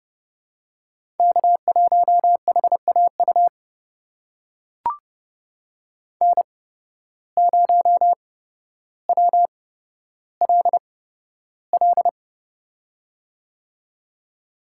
Morse code, heard as a keyed tone, about 29 wpm, spaced as for 13 wpm, 700 Hz. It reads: K1HAU N0WLL
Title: USA Call Signs - 5 Characters - 8x Character Spacing - 30wpm